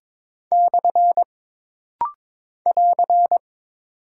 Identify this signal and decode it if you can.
Morse code, heard as a keyed tone, about 22 wpm, 700 Hz. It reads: / <AR>
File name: practice / 22wpm / Single Punctuation-Prosign - Rapid-Fire - 22wpm